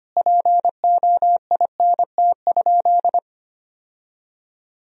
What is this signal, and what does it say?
Morse code, heard as a keyed tone, about 25 wpm, 700 Hz. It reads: POINT?
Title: Sentences from Top 1000 Words - Review Part 1 - 25wpm